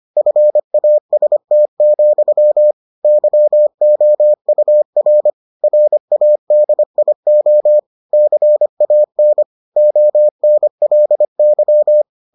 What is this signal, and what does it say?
Morse code, heard as a keyed tone, about 25 wpm, 600 Hz. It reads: FAST, YOUR RADIO CAN ONLY